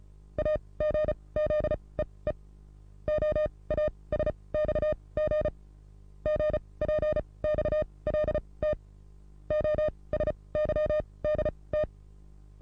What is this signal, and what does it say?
Morse code, heard as a keyed tone, about 35 wpm, spaced as for 23 wpm, 610 Hz. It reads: AGZEE OASXG GPXLT OSYDT
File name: repos / Morsekurs / DJ4UF Morsen27